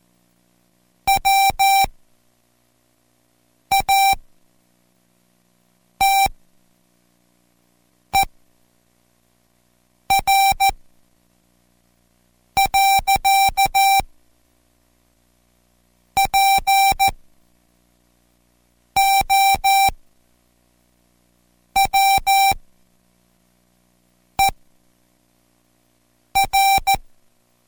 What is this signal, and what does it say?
Morse code, heard as a keyed tone, about 14 wpm, spaced as for 4 wpm, 780 Hz. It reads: WATER. POWER